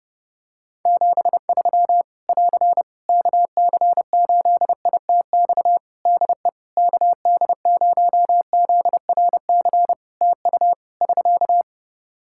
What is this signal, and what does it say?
Morse code, heard as a keyed tone, about 29 wpm, 700 Hz. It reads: 73 <AR> KC8ITX DE KD0ZRC TU <SK>
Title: Practice QSO - Exchange 4 - Ending - 30wpm